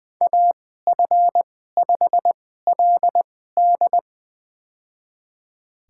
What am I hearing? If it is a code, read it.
Morse code, heard as a keyed tone, about 21 wpm, spaced as for 15 wpm, 700 Hz. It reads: AF5LD
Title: USA Call Signs - 2x Character Spacing - 20wpm